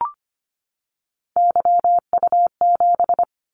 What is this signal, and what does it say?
Morse code, heard as a keyed tone, about 25 wpm, 700 Hz. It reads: YU7
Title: Callsign Prefixes for Top 25 DXCC - 25wpm